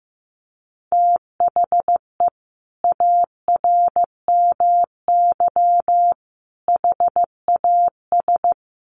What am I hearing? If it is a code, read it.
Morse code, heard as a keyed tone, about 15 wpm, 700 Hz. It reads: THE ARMY HAS
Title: Sentences from Top 1400 Words 15wpm